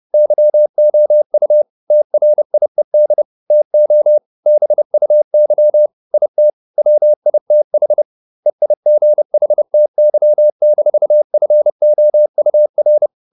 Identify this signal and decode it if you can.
Morse code, heard as a keyed tone, about 30 wpm, 600 Hz. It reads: YOU TRIED TO BUY IT WITH EIGHTY-FOUR